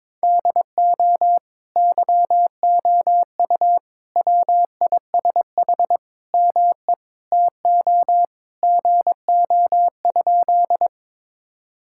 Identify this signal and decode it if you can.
Morse code, heard as a keyed tone, about 22 wpm, 700 Hz. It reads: DO YOU WISH ME TO GO?
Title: Sentences from Top 1600 Words 22wpm